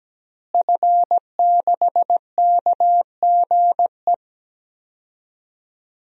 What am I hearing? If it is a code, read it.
Morse code, heard as a keyed tone, about 17 wpm, 700 Hz. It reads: F6KGE
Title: DXCC-100 Call Signs - 5 Characters - Repeated 2x - 17wpm